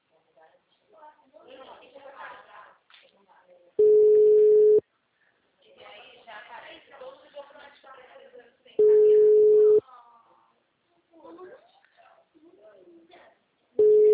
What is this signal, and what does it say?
Ringback tone (Europe, 425 Hz), with speech.
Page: Pavuna - RJ